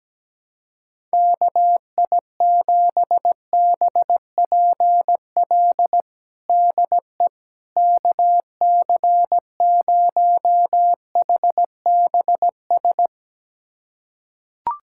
Morse code, heard as a keyed tone, about 17 wpm, 700 Hz. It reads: KI7BPL DE KC0HBS
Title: USA Callsign DE Callsign - 6 Characters - 17wpm